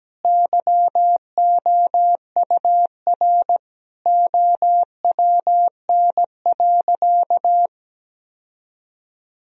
Morse code, heard as a keyed tone, about 17 wpm, 700 Hz. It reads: YOUR OWN.
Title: Sentences from Top 1000 Words - Review Part 2 - 17wpm